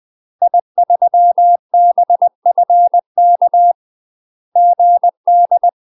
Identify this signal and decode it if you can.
Morse code, heard as a keyed tone, about 20 wpm, 700 Hz. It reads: I3BFK GD
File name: UK MASTER QSOs TRAINING/20WPM